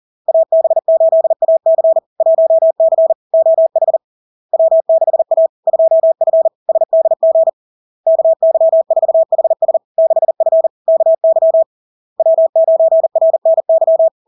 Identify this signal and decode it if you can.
Morse code, heard as a keyed tone, about 40 wpm, 650 Hz. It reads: AB8AC 1C OH W6A 2F SDG KY4HS 6F KY W9RNY
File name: LARK-FD-2022-Log-in-CW-40wpm-650-Hz